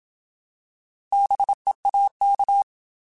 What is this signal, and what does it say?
Morse code, heard as a keyed tone, about 27 wpm, 800 Hz. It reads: BEAK
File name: Alphabetical / Alphabetical mp3s / Bs @25